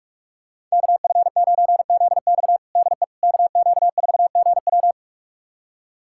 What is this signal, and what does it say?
Morse code, heard as a keyed tone, about 46 wpm, 700 Hz. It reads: KF9ZX DE KQ4GW
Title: USA Callsign DE Callsign - 5 Characters - 45wpm